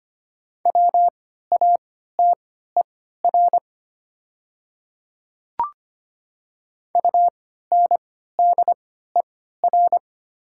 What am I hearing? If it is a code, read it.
Morse code, heard as a keyed tone, about 25 wpm, spaced as for 14 wpm, 700 Hz. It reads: WATER UNDER
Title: 3x Character Spacing - Top 300 Words - 25wpm